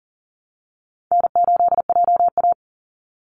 Morse code, heard as a keyed tone, about 40 wpm, 700 Hz. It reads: N8JA
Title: Easy Call Signs - 40wpm